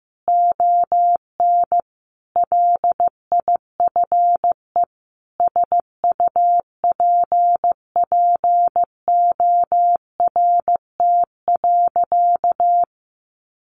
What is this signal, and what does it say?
Morse code, heard as a keyed tone, about 15 wpm, 700 Hz. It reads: ON LIFE SUPPORT.
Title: Sentences from Top 1300 Words 15wpm